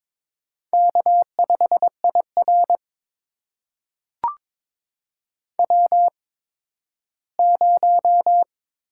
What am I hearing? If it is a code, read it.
Morse code, heard as a keyed tone, about 22 wpm, spaced as for 13 wpm, 700 Hz. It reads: K5IR W0
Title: USA Call Signs - 4 Characters - 8x Character Spacing - 22wpm